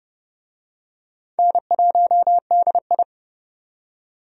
Morse code, heard as a keyed tone, about 32 wpm, 700 Hz. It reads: N1DI
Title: USA Call Signs - 4 Characters - 30wpm